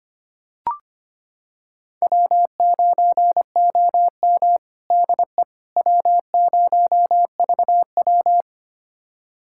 Morse code, heard as a keyed tone, about 25 wpm, 700 Hz. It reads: W9OM DE W0VW
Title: USA Callsign DE Callsign - 4 Characters - 25wpm